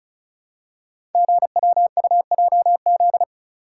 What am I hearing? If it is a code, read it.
Morse code, heard as a keyed tone, about 34 wpm, 700 Hz. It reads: GWUJZ
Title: Code Groups - 5 Letters - 35wpm